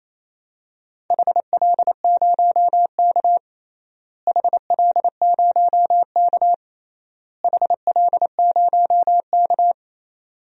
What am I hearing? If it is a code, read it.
Morse code, heard as a keyed tone, about 28 wpm, 700 Hz. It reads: HL0K HL0K HL0K
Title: DXCC-100 Call Signs - 4 Characters - Repeated 3x - 28wpm